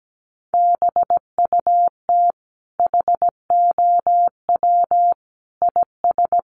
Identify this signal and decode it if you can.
Morse code, heard as a keyed tone, about 17 wpm, 700 Hz. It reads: BUT HOW IS
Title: Sentences from Top 100 Words 17wpm